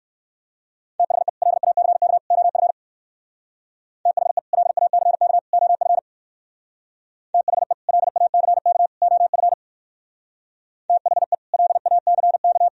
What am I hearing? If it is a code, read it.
Morse code, heard as a keyed tone, about 58 wpm, 700 Hz. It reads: THE LACK OF THE LACK OF THE LACK OF THE LACK